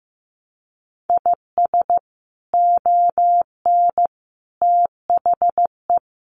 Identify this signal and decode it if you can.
Morse code, heard as a keyed tone, about 15 wpm, 700 Hz. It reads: IS ON THE
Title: Sets of 3 Words 15wpm